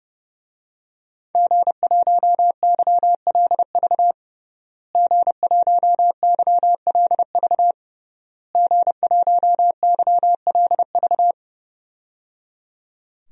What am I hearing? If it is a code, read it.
Morse code, heard as a keyed tone, about 30 wpm, 700 Hz. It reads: G1YLV G1YLV G1YLV
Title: DXCC-100 Call Signs - 5 Characters - Repeated 3x - 30wpm